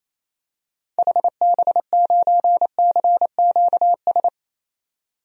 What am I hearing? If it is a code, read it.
Morse code, heard as a keyed tone, about 28 wpm, 700 Hz. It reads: HB9CQS
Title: DXCC-100 Call Signs - 6 Characters - 28wpm